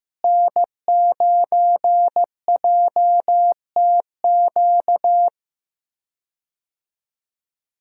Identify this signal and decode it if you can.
Morse code, heard as a keyed tone, about 15 wpm, 700 Hz. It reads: N9JTQ
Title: USA Call Signs - 5 Characters - Repeated 3x - 15wpm